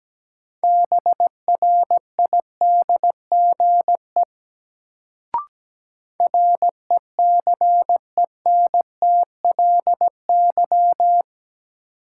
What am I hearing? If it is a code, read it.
Morse code, heard as a keyed tone, about 17 wpm, 700 Hz. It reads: BRIDGE RECENTLY
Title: Sentences from Top 1200 Words - New Word Drill - 17wpm